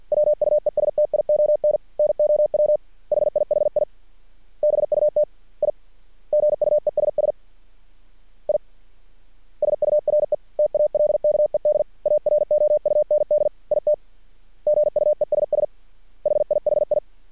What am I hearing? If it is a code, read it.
Morse code, heard as a keyed tone, about 50 wpm, 600 Hz. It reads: WUESTION NOW HIHI BUT I GUESS I SURE TALKED AROUND IT GUESS HIHI
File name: archived / CRSnet